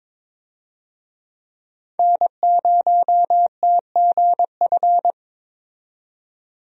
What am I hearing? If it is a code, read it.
Morse code, heard as a keyed tone, about 22 wpm, 700 Hz. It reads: N0TGF
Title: USA Call Signs - 5 Characters - 22wpm